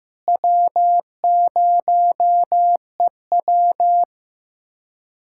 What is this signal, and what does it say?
Morse code, heard as a keyed tone, about 15 wpm, 700 Hz. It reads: W0EW
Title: USA Callsign DE Callsign - 4 Characters - 15wpm